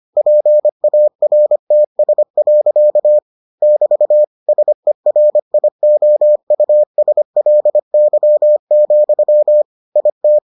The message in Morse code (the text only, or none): PARTS. <BT> SERIOUSLY, IT